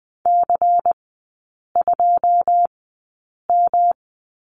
Morse code, heard as a keyed tone, about 20 wpm, spaced as for 8 wpm, 700 Hz. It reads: C2M
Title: 178 Lesson 29e - Slash - Three Characters 20wpm